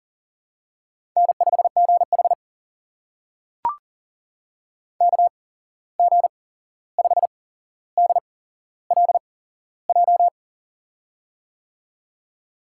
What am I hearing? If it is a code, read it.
Morse code, heard as a keyed tone, about 42 wpm, spaced as for 15 wpm, 700 Hz. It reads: N5GH KG5DLJ
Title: USA Call Signs - 8x Character Spacing - 40wpm